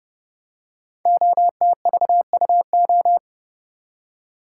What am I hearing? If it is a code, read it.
Morse code, heard as a keyed tone, about 30 wpm, 700 Hz. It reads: OTVUO
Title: Code Groups - 5 Letter:Numbers - 30wpm